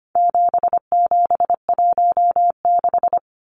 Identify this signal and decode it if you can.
Morse code, heard as a keyed tone, about 25 wpm, 700 Hz. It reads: 7716
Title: Numbers - 4-Digit 25wpm